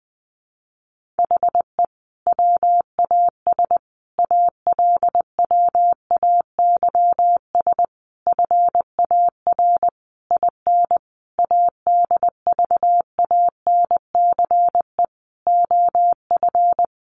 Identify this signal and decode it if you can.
Morse code, heard as a keyed tone, about 20 wpm, 700 Hz. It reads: HE WAS ALWAYS FAR IN ADVANCE OF
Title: Sentences from Top 1400 Words 20wpm